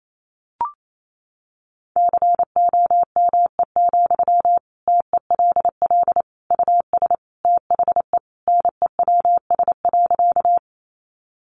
Morse code, heard as a keyed tone, about 28 wpm, 700 Hz. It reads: COME, TELL US THE NEWS.
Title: Sentences from Top 800 Words 28wpm